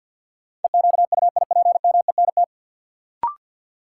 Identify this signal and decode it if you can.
Morse code, heard as a keyed tone, about 47 wpm, 700 Hz. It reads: EQUIPMENT
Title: Sentences from Top 700 Words - New Word Drill - 50wpm